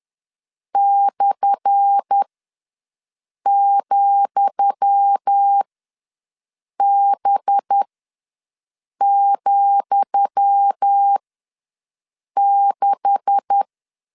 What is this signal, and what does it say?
Morse code, heard as a keyed tone, about 10 wpm, spaced as for 5 wpm, 800 Hz. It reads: /,B,6
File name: LEZIONI / 10 SIGNS / 10D COMA ws 10wpm 4314